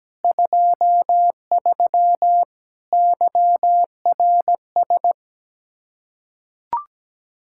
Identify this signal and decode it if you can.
Morse code, heard as a keyed tone, about 17 wpm, 700 Hz. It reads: 23 YRS
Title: Practice QSO - Exchange 3 - More Info - 17wpm